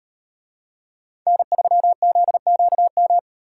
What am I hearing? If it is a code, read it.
Morse code, heard as a keyed tone, about 38 wpm, 700 Hz. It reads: N3ZQM